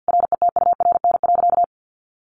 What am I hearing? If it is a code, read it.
Morse code, heard as a keyed tone, about 50 wpm, 700 Hz. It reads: RETURN.
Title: Sentences from Top 800 Words 50wpm